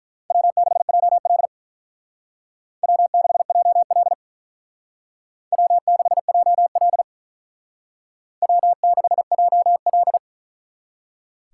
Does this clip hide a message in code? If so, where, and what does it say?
Yes: Morse code from 0.3 to 10.2 s, heard as a keyed tone, about 44 wpm, 700 Hz. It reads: W6JL W6JL W6JL W6JL